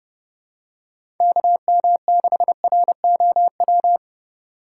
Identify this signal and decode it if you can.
Morse code, heard as a keyed tone, about 29 wpm, 700 Hz. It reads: KM6ROW
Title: USA Call Signs - 3x Character Spacing - 30wpm